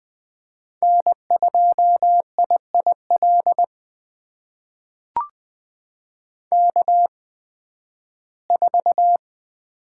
Morse code, heard as a keyed tone, about 20 wpm, spaced as for 13 wpm, 700 Hz. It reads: N2IIL K4
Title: USA Call Signs - 5 Characters - 8x Character Spacing - 20wpm